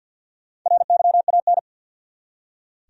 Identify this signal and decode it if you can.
Morse code, heard as a keyed tone, about 46 wpm, 700 Hz. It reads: RYAN
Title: USA Common Names - 50wpm